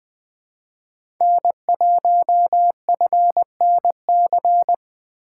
Morse code, heard as a keyed tone, about 21 wpm, 700 Hz. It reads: N1FNC
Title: USA Call Signs - 5 Characters - 4x Character Spacing - 20wpm